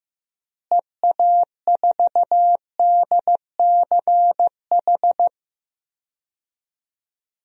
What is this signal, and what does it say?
Morse code, heard as a keyed tone, about 16 wpm, 700 Hz. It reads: EA4DCH
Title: DXCC-100 Call Signs - 6 Characters - Repeated 2x - 15wpm